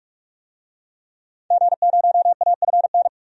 Morse code, heard as a keyed tone, about 44 wpm, 700 Hz. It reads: G0AFN
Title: DXCC-100 Call Signs - 5 Characters - 45wpm